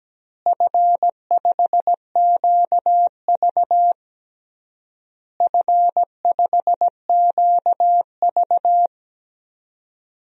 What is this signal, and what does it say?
Morse code, heard as a keyed tone, about 17 wpm, 700 Hz. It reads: F5QV F5QV
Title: DXCC-100 Call Signs - 4 Characters - Repeated 3x - 17wpm